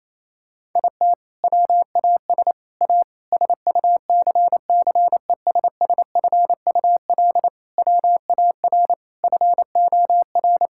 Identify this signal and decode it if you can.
Morse code, heard as a keyed tone, about 28 wpm, 700 Hz. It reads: IT WAS A SUCCESSFUL WAR FOR
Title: Sentences from Top 1000 Words - Review Part 6 - 28wpm